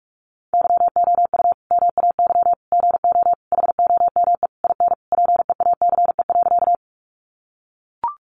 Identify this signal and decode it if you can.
Morse code, heard as a keyed tone, about 44 wpm, 700 Hz. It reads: YOU MAY GO HOME IN PEACE.
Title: Sentences from Top 1500 Words 45wpm